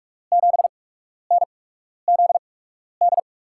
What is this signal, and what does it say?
Morse code, heard as a keyed tone, about 44 wpm, spaced as for 12 wpm, 700 Hz. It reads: 7NZD
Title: USA Call Signs - 6 Characters - 8x Character Spacing - 45wpm